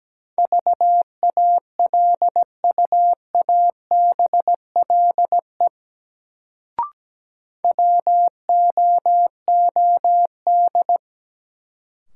Morse code, heard as a keyed tone, about 17 wpm, 700 Hz. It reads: VALUABLE WOOD
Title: Sentences from Top 1100 Words - New Word Drill Alt - 17wpm